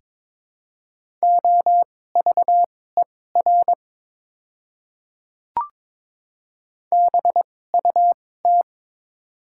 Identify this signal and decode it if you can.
Morse code, heard as a keyed tone, about 22 wpm, spaced as for 16 wpm, 700 Hz. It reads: OVER BUT
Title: 2x Character Spacing - Top 500 Words - 22wpm